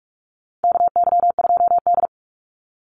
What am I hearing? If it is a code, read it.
Morse code, heard as a keyed tone, about 44 wpm, 700 Hz. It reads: KY2D
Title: Call Signs - 45wpm